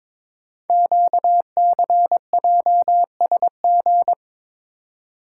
Morse code, heard as a keyed tone, about 23 wpm, 700 Hz. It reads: QCJSG
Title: Code Groups - 5 Letters - 22wpm